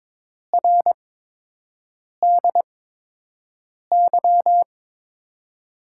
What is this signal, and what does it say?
Morse code, heard as a keyed tone, about 22 wpm, spaced as for 6 wpm, 700 Hz. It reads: RDY
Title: USA Call Signs - 6 Characters - 8x Character Spacing - 22wpm